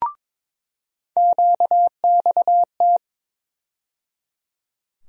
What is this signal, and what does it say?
Morse code, heard as a keyed tone, about 22 wpm, 700 Hz. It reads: QXT
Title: USA Call Sign Postfixes - 3 Letters 22wpm